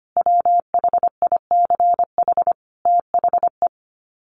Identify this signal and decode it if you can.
Morse code, heard as a keyed tone, about 25 wpm, 700 Hz. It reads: WHICH THE
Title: Sets of 5 Words Encore2 25wpm